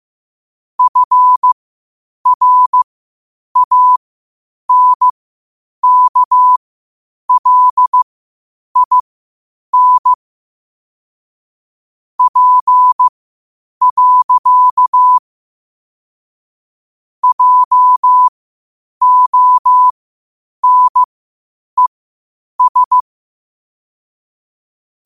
Morse code, heard as a keyed tone, about 15 wpm, spaced as for 8 wpm, 1000 Hz. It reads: FRANKLIN P. JONES
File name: qotdfiles / 8wpm